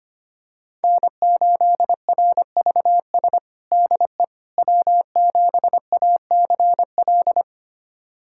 Morse code, heard as a keyed tone, about 25 wpm, 700 Hz. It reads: N8RVS DE W7ACL
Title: USA Callsign DE Callsign - 5 Characters - 25wpm